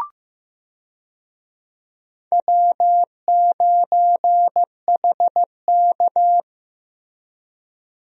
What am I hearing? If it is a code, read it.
Morse code, heard as a keyed tone, about 15 wpm, 700 Hz. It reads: W9HK
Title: USA Call Signs - 4 Characters - Repeated 2x - 15wpm